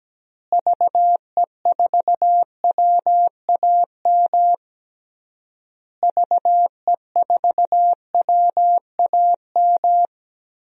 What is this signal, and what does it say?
Morse code, heard as a keyed tone, about 17 wpm, 700 Hz. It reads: VE4WAM VE4WAM
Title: DXCC-100 Call Signs - 6 Characters - Repeated 3x - 17wpm